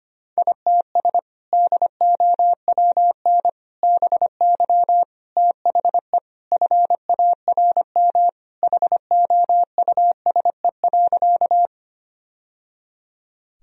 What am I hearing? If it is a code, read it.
Morse code, heard as a keyed tone, about 25 wpm, 700 Hz. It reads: ITS DOWN BY THE FARM HOUSE.
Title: Sentences from Top 1000 Words - Review Part 2 - 25wpm